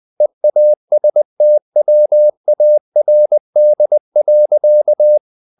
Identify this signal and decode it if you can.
Morse code, heard as a keyed tone, about 20 wpm, 600 Hz. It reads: EASTWARD.
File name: the old man and the sea / the old man and the sea part 1 20wpm